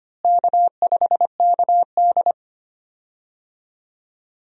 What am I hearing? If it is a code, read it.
Morse code, heard as a keyed tone, about 25 wpm, 700 Hz. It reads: K5KD
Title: USA Call Signs - 4 Characters - Repeated 2x - 25wpm